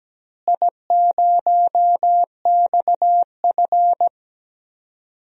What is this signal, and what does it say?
Morse code, heard as a keyed tone, about 17 wpm, 700 Hz. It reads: I0XF
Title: DXCC-100 Call Signs - 4 Characters - Repeated 2x - 17wpm